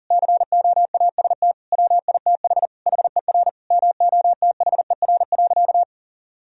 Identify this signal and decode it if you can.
Morse code, heard as a keyed tone, about 40 wpm, 700 Hz. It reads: COAST WITH HER MOTHER.
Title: Sentences from Top 1000 Words - Review Part 4 - 40wpm